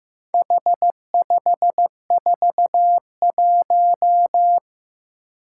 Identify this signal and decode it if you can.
Morse code, heard as a keyed tone, about 15 wpm, 700 Hz. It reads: H541
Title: Four Challenging Letter-Number Combinations 15wpm